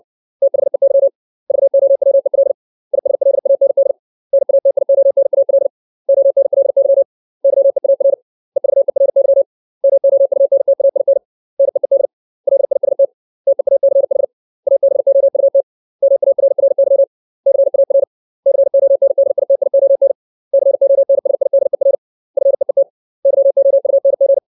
Morse code, heard as a keyed tone, about 60 wpm, 550 Hz. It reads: THEY VOWEL ISLAND NATIONAL ONLY YEAR EVERY MOUNTAIN NEED LIST TEACH ABOUT MARRY CAR CONDITION CONSIDER FEET YOUNG
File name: 632 words / 632 Words 60WPM